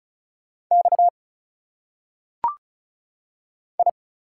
Morse code, heard as a keyed tone, about 36 wpm, 700 Hz. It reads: X I
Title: Single Letters - Phonetic Pronunciation 35wpm